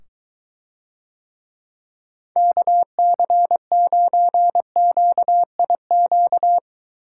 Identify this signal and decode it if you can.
Morse code, heard as a keyed tone, about 23 wpm, 700 Hz. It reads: KC9QIQ